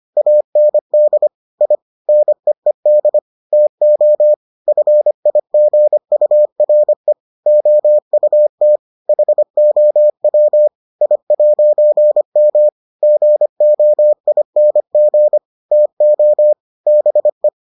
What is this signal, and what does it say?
Morse code, heard as a keyed tone, about 25 wpm, 600 Hz. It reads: AND I NEED TO FIGURE OUT HOW I'M GOING TO BE